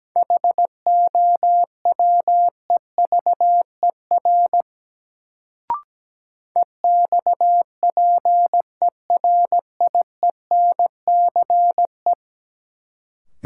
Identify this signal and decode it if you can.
Morse code, heard as a keyed tone, about 17 wpm, 700 Hz. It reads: HOWEVER EXPERIENCE
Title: Sentences from Top 600 Words - New Word Drill - 17wpm